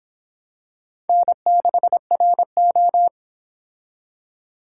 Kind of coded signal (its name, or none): Morse code